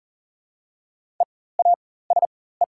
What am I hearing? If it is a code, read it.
Morse code, heard as a keyed tone, about 41 wpm, spaced as for 19 wpm, 700 Hz. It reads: EASE
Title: 4x Character Spacing - Top 300 Words - 40wpm